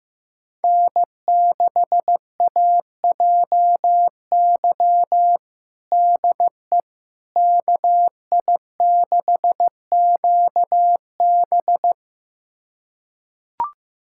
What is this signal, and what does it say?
Morse code, heard as a keyed tone, about 15 wpm, 700 Hz. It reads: N6AJY DE KI6QB
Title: USA Callsign DE Callsign - 5 Characters - 15wpm